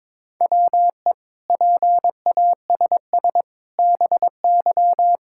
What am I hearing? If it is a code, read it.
Morse code, heard as a keyed tone, about 22 wpm, 700 Hz. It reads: WE PASS BY
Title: Sentences from Top 1600 Words 22wpm